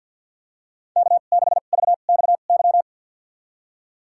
Morse code, heard as a keyed tone, about 53 wpm, spaced as for 37 wpm, 700 Hz. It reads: K6VXY
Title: USA Call Signs - 5 Characters - 2x Character Spacing - 50wpm